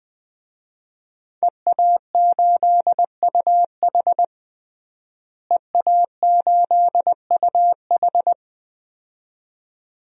Morse code, heard as a keyed tone, about 20 wpm, 700 Hz. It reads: EA8UH EA8UH
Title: DXCC-100 Call Signs - 5 Characters - Repeated 2x - 20wpm